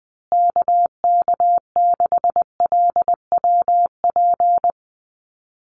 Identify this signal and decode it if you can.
Morse code, heard as a keyed tone, about 20 wpm, 700 Hz. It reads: KK6LWP
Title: USA Call Signs Repeated 2x 20wpm